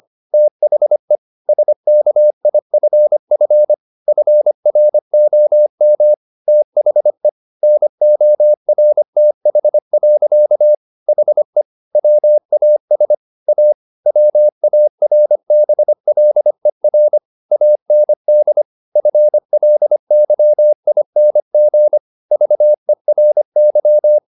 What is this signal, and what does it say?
Morse code, heard as a keyed tone, about 25 wpm, 600 Hz. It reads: THE SKIFF FROM THE NORTH. HE WAS A WARBLER AND FLYING VERY
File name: the old man and the sea part 1 25wpm